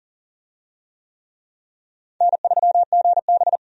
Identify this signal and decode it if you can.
Morse code, heard as a keyed tone, about 40 wpm, 700 Hz. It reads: N3GB
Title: USA Call Signs - 4 Characters - 40wpm